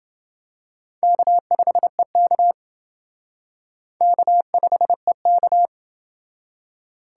Morse code, heard as a keyed tone, about 29 wpm, 700 Hz. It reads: K5EK K5EK